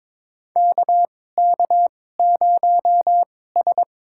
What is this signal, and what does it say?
Morse code, heard as a keyed tone, about 22 wpm, spaced as for 16 wpm, 700 Hz. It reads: KK0S
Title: USA Call Signs - 4 Characters - 2x Character Spacing - 22wpm